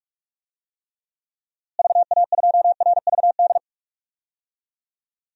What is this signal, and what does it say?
Morse code, heard as a keyed tone, about 46 wpm, 700 Hz. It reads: VA2RVD
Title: DXCC-100 Call Signs - 6 Characters - Repeated 2x - 45wpm